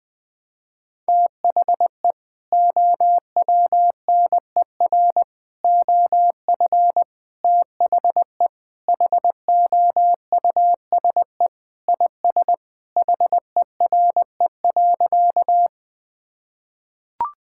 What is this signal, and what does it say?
Morse code, heard as a keyed tone, about 20 wpm, 700 Hz. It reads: THE OWNER OF THE HOUSE IS HERE.
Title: Sentences with 3-Word Phrases - 20wpm